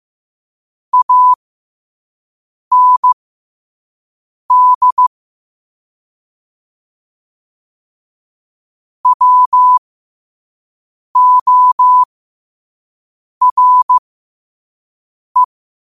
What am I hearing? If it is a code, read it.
Morse code, heard as a keyed tone, about 15 wpm, spaced as for 5 wpm, 1000 Hz. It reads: AND WORE
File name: qotdfiles / 5wpm